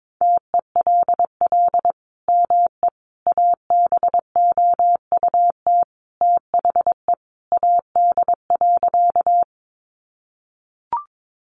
Morse code, heard as a keyed tone, about 22 wpm, 700 Hz. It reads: TELL ME ABOUT THE AD.
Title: Sentences from Top 1300 Words 22wpm